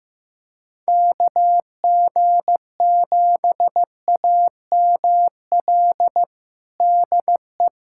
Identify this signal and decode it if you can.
Morse code, heard as a keyed tone, about 15 wpm, 700 Hz. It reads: KG7AML DE
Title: USA Callsign DE Callsign 15wpm